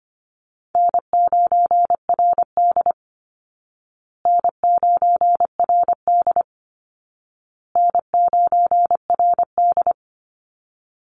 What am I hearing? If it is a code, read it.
Morse code, heard as a keyed tone, about 25 wpm, 700 Hz. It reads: N9RD N9RD N9RD
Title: Call Signs Repeated 2x - 25wpm